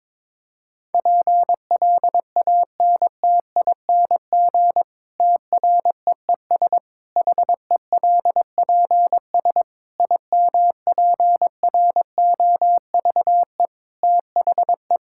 Morse code, heard as a keyed tone, about 22 wpm, 700 Hz. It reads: PLANTING TREES HELPS IMPROVE THE
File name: Difficult Sentences with Hints - 22wpm